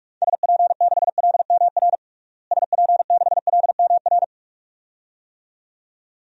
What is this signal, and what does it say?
Morse code, heard as a keyed tone, about 46 wpm, 700 Hz. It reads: SP6LMR SP6LMR
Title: DXCC-100 Call Signs - 6 Characters - Repeated 2x - 45wpm